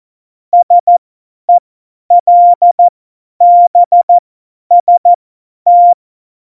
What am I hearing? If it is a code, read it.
Morse code, heard as a keyed tone, about 14 wpm, spaced as for 10 wpm, 700 Hz. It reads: SELBST
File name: Morsekurs / Witze / Witz 21. Mai 2019, 50 F70